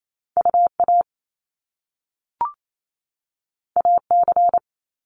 Morse code, heard as a keyed tone, about 28 wpm, 700 Hz. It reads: UA AC
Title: Top Two Letter Combinations in Words 28wpm